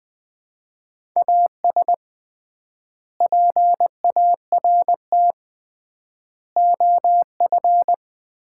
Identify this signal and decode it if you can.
Morse code, heard as a keyed tone, about 20 wpm, 700 Hz. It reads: AS PART OF
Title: 3x Word-Spacing - Sets of 4 Words - 20wpm